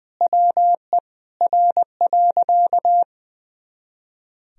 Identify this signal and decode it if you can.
Morse code, heard as a keyed tone, about 21 wpm, 700 Hz. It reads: WE R.
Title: Sentences from Top 200 Words and Abbreviations 20wpm